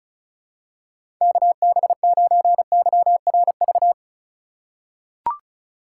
Morse code, heard as a keyed tone, about 36 wpm, 700 Hz. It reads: KB9YRV
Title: USA Call Signs - 6 Characters - 2x Character Spacing - 35wpm